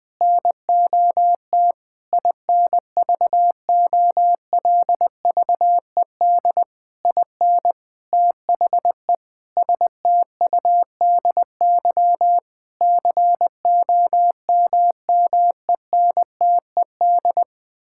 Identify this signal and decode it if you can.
Morse code, heard as a keyed tone, about 20 wpm, 700 Hz. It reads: NOT INVOLVED IN THE STUDY COMMENTED
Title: Sentences with 7-Word Phrases - Spoken Beginning - 20wpm